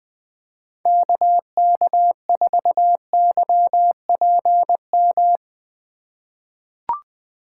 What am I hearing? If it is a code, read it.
Morse code, heard as a keyed tone, about 21 wpm, 700 Hz. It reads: KK4YPM
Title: USA Call Signs - 6 Characters - 4x Character Spacing - 20wpm